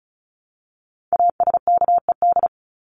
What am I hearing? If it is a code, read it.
Morse code, heard as a keyed tone, about 34 wpm, 700 Hz. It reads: ASKED
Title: ED Suffix - Top 100 Words with Base Words 35wpm